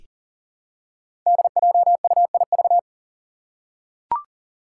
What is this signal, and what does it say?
Morse code, heard as a keyed tone, about 40 wpm, 700 Hz. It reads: DJUIV
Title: Five Challenging Letter-Number Combinations 40wpm